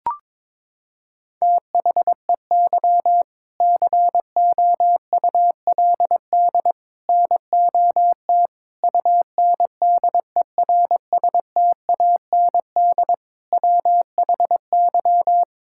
Morse code, heard as a keyed tone, about 22 wpm, 700 Hz. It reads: THEY COULD NOT UNDERSTAND WHY
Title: Sentences from Top 1000 Words - Review Part 6 - 22wpm